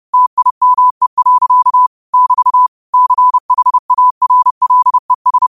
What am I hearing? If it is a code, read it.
Morse code, heard as a keyed tone, about 30 wpm, 1000 Hz. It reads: TIME. <BT> CHARLES
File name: qotdfiles / 30wpm